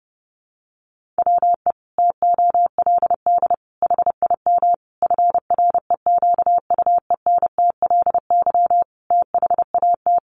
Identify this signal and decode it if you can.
Morse code, heard as a keyed tone, about 30 wpm, 700 Hz. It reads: WE TOLD HIM FREQUENTLY THAT
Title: Sentences from Top 1300 Words 30wpm